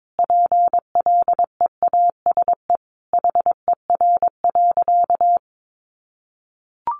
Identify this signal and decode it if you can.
Morse code, heard as a keyed tone, about 22 wpm, 700 Hz. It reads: PLEASE HER.
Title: Sentences from Top 1500 Words 22wpm